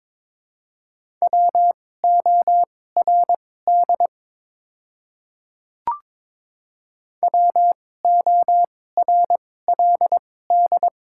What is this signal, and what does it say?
Morse code, heard as a keyed tone, about 22 wpm, spaced as for 16 wpm, 700 Hz. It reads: WORD WORLD
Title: 2x Character Spacing - Top 200 Words - 22wpm